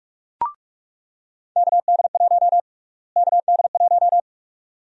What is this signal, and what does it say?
Morse code, heard as a keyed tone, about 46 wpm, 700 Hz. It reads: KD1 KD1
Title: USA Call Sign Prefixes - Repeated 2x - 45wpm